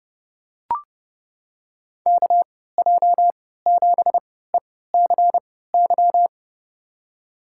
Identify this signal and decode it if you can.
Morse code, heard as a keyed tone, about 32 wpm, spaced as for 17 wpm, 700 Hz. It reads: KJ7ECY
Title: USA Call Signs - 6 Characters - 3x Character Spacing - 30wpm